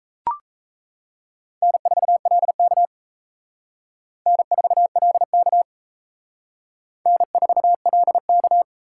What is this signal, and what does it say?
Morse code, heard as a keyed tone, about 34 wpm, 700 Hz. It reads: N4LK N4LK N4LK
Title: Speed Racing - USA 1x2 Call Signs 28wpm